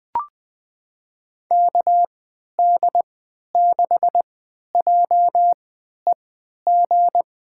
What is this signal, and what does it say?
Morse code, heard as a keyed tone, about 20 wpm, spaced as for 11 wpm, 700 Hz. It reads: KD6JEG
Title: USA Call Signs - 6 Characters - 3x Character Spacing - 20wpm